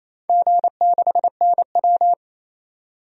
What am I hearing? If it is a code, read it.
Morse code, heard as a keyed tone, about 28 wpm, 700 Hz. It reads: G6NW
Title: DXCC-100 Call Signs - 4 Characters - 28wpm